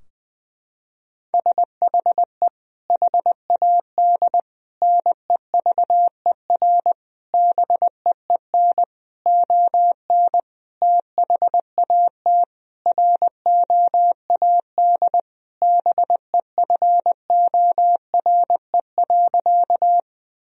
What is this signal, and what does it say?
Morse code, heard as a keyed tone, about 21 wpm, 700 Hz. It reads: SHE HAD NEVER BEEN ON THAT ROAD BEFORE.
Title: Sentences from Top 1000 Words - Review Part 2 - 20wpm